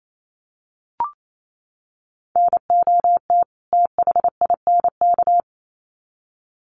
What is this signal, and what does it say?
Morse code, heard as a keyed tone, about 28 wpm, 700 Hz. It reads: NOT THINK
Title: Sets of 2 Words 28wpm